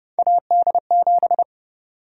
Morse code, heard as a keyed tone, about 30 wpm, 700 Hz. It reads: AD7
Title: USA Call Sign Prefixes - 30wpm